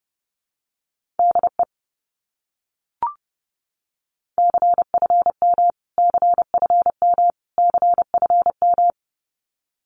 Morse code, heard as a keyed tone, about 30 wpm, 700 Hz. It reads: DE CFM CFM CFM
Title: Abbreviations Repeated 3x - 30wpm